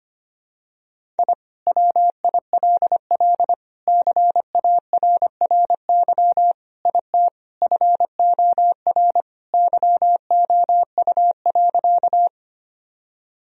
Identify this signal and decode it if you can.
Morse code, heard as a keyed tone, about 25 wpm, 700 Hz. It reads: I WILL CARRY IT FOR YOU.
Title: Sentences from Top 1000 Words - Review Part 1 - 25wpm